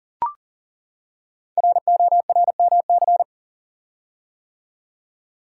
Morse code, heard as a keyed tone, about 40 wpm, 700 Hz. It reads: RORMC
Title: Code Groups - 5 Letters - 40wpm